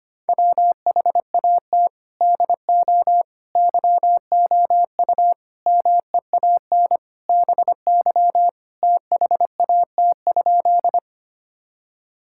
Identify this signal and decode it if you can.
Morse code, heard as a keyed tone, about 25 wpm, 700 Hz. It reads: WHAT DO YOU MEAN BY THAT?
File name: practice / 25wpm / Sentences with 4-Word Phrases - 25wpm